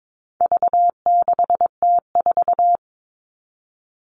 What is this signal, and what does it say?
Morse code, heard as a keyed tone, about 22 wpm, 700 Hz. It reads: V6T4
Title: All Cut Numbers - 4-Digit 22wpm